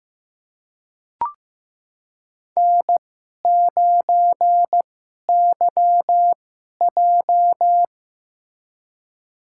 Morse code, heard as a keyed tone, about 15 wpm, spaced as for 11 wpm, 700 Hz. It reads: N9YJ
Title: USA Call Signs - 4 Characters - 2x Character Spacing - 15wpm